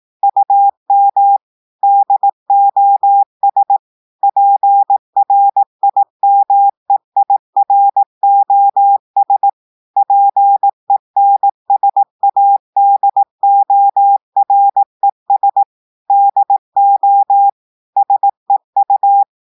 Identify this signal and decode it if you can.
Morse code, heard as a keyed tone, about 18 wpm, 800 Hz. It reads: UM DOS PRIMEIROS PENSADORES DO SEU
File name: audios / portugais / portugais texte en clair 1 vitesse 18 mots minutes